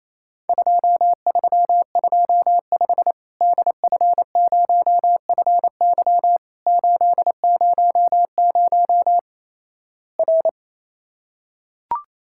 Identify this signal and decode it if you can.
Morse code, heard as a keyed tone, about 28 wpm, 700 Hz. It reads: 2325 DF0FY 800 R
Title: Worked All Europe DX Contest - QTC Exchange - 28wpm